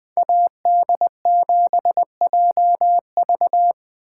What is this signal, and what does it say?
Morse code, heard as a keyed tone, about 20 wpm, 700 Hz. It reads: AD7JV